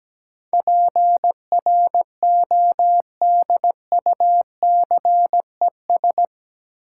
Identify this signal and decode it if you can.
Morse code, heard as a keyed tone, about 17 wpm, 700 Hz. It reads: PRODUCES
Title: Sentences from Top 1000 Words - Review Part 6 - New Word Drill - 17wpm